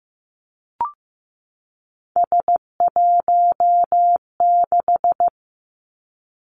Morse code, heard as a keyed tone, about 15 wpm, 700 Hz. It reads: S16
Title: Three Challenging Letter-Number Combinations 15wpm